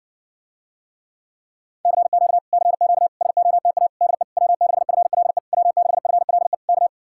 Morse code, heard as a keyed tone, about 69 wpm, 700 Hz. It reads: CQ CQ SOTA DE W6FLE W6FLE K